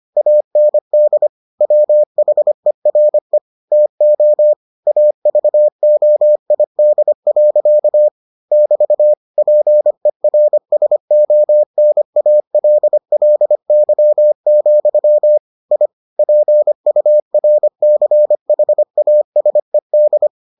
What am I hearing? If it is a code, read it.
Morse code, heard as a keyed tone, about 25 wpm, 600 Hz. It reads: AND WHERE TO AVOID. <BT> PERSONALLY, I PURCHASED